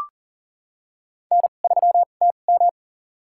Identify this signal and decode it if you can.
Morse code, heard as a keyed tone, about 40 wpm, spaced as for 29 wpm, 700 Hz. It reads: N3TM
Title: USA Call Signs - 4 Characters - 2x Character Spacing - 40wpm